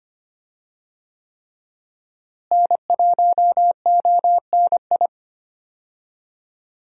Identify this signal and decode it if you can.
Morse code, heard as a keyed tone, about 25 wpm, 700 Hz. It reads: N1ONI
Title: USA Call Signs - 5 Characters - Repeated 3x - 25wpm